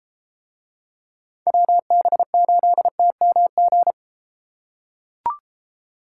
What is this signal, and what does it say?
Morse code, heard as a keyed tone, about 31 wpm, 700 Hz. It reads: WB8TMG